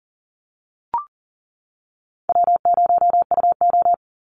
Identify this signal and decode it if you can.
Morse code, heard as a keyed tone, about 41 wpm, 700 Hz. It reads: W0UO
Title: Call Signs - 40wpm